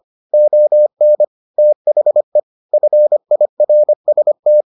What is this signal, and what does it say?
Morse code, heard as a keyed tone, about 25 wpm, 600 Hz. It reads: ON THE FIRST